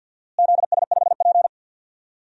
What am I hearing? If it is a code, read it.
Morse code, heard as a keyed tone, about 53 wpm, 700 Hz. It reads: ZS5P
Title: DXCC-100 Call Signs - 4 Characters - Repeated 2x - 50wpm